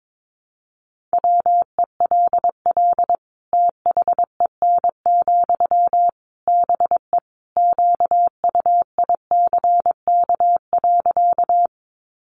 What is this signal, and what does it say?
Morse code, heard as a keyed tone, about 22 wpm, 700 Hz. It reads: WELL THEN, BE QUICK.
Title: Sentences from Top 500 Words 22wpm